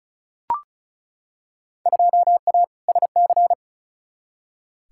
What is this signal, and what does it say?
Morse code, heard as a keyed tone, about 36 wpm, 700 Hz. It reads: 2A SC
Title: Field Day Exchanges - 35wpm